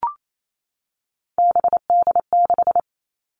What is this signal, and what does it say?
Morse code, heard as a keyed tone, about 28 wpm, 700 Hz. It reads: BD6
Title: All Cut Numbers - 3-Digit 28wpm